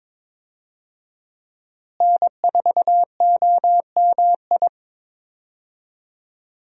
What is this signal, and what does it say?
Morse code, heard as a keyed tone, about 22 wpm, 700 Hz. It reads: N4OMI
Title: USA Call Signs - 5 Characters - Repeated 2x - 22wpm